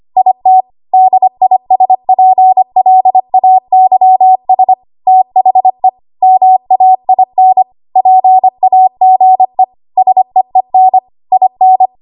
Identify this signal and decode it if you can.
Morse code, heard as a keyed tone, about 25 wpm, 750 Hz. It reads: IT DISPLAYS THE MAIN PAGE SEEN IN
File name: Archive / 25 WPM / 25WPM